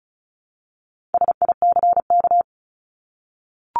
Morse code, heard as a keyed tone, about 34 wpm, 700 Hz. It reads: SICK
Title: Top 1501-1750 Words 35wpm